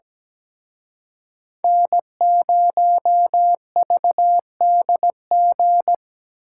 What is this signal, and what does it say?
Morse code, heard as a keyed tone, about 20 wpm, 700 Hz. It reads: N0VDG
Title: Speed Racing - USA 1x3 Call Signs 17wpm